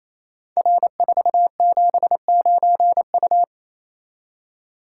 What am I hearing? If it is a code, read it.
Morse code, heard as a keyed tone, about 29 wpm, 700 Hz. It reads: R479U
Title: Code Groups - 5 Letter:Numbers - 28wpm